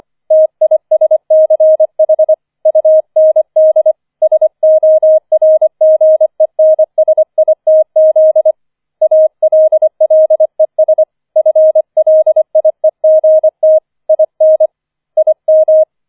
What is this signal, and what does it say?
Morse code, heard as a keyed tone, about 25 wpm, 630 Hz. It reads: TISCH UND SORGENSITZ ALLES FLIEGT IN IM
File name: 004 Vierter Streich 25wpm